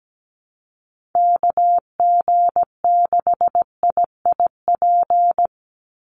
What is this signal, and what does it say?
Morse code, heard as a keyed tone, about 17 wpm, 700 Hz. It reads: KG6IIP
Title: Call Signs Repeated 2x - 17wpm